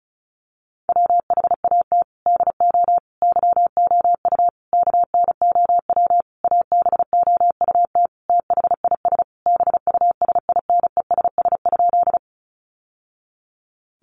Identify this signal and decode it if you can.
Morse code, heard as a keyed tone, about 34 wpm, 700 Hz. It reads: WHAT DO YOU KNOW ABOUT THIS BUSINESS?
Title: Sentences from Top 600 Words 35wpm